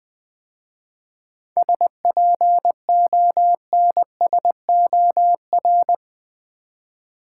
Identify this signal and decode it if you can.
Morse code, heard as a keyed tone, about 20 wpm, 700 Hz. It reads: SPONSOR
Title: 7 Letter Words - First 3 Letters Given - 20wpm